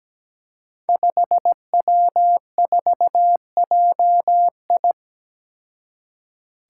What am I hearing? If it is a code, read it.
Morse code, heard as a keyed tone, about 17 wpm, 700 Hz. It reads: 5W4JI
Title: Five Challenging Letter-Number Combinations 17wpm